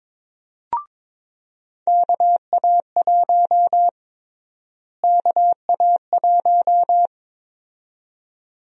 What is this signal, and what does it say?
Morse code, heard as a keyed tone, about 22 wpm, 700 Hz. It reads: KA1 KA1
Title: USA Call Sign Prefixes - Repeated 2x - 22wpm